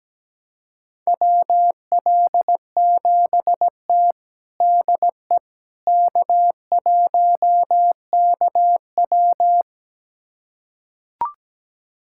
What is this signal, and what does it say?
Morse code, heard as a keyed tone, about 17 wpm, 700 Hz. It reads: WL7T DE K1KW
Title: USA Callsign DE Callsign - 4 Characters - 17wpm